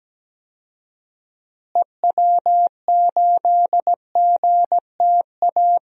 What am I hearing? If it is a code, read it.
Morse code, heard as a keyed tone, about 17 wpm, 700 Hz. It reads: EW8GTA
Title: DXCC-100 Call Signs - 6 Characters - Repeated 3x - 17wpm